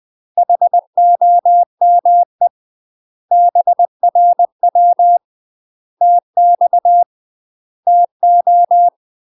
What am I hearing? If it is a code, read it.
Morse code, heard as a keyed tone, about 20 wpm, 700 Hz. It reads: HOME BRW TX TO
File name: UK MASTER QSOs TRAINING / 20WPM / UK MASTER QSOs 0062